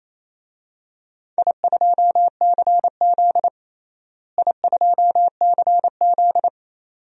Morse code, heard as a keyed tone, about 28 wpm, 700 Hz. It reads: I2CZ I2CZ
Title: DXCC-100 Call Signs - 4 Characters - Repeated 2x - 28wpm